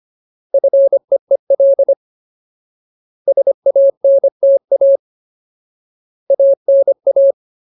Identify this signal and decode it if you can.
Morse code, heard as a keyed tone, about 25 wpm, 550 Hz. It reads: FEEL SANTA ANA
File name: Ultimate Word List / 25WPM Ultimate